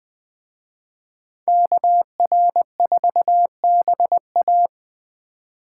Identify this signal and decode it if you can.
Morse code, heard as a keyed tone, about 20 wpm, 700 Hz. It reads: KR4BA